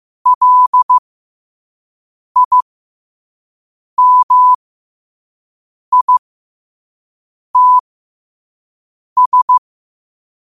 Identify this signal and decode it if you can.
Morse code, heard as a keyed tone, about 15 wpm, spaced as for 5 wpm, 1000 Hz. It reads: LIMITS
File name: qotdfiles / 5wpm